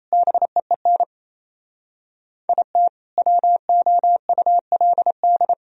Morse code, heard as a keyed tone, about 37 wpm, 700 Hz. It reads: BEEN IT WOULD